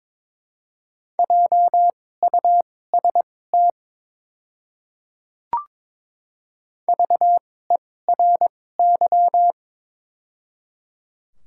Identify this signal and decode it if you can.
Morse code, heard as a keyed tone, about 23 wpm, spaced as for 16 wpm, 700 Hz. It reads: JUST VERY
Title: 2x Character Spacing - Top 200 Words - 22wpm